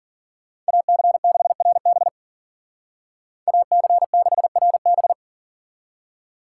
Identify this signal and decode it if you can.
Morse code, heard as a keyed tone, about 50 wpm, 700 Hz. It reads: AC6RB AC6RB
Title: Speed Racing - USA 2x2 Call Signs 40wpm